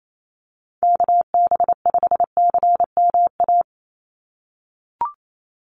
Morse code, heard as a keyed tone, about 28 wpm, 700 Hz. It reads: KB5CMA
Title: USA Call Signs 28wpm